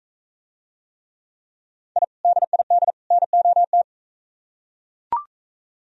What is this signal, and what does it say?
Morse code, heard as a keyed tone, about 41 wpm, 700 Hz. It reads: I DID NOT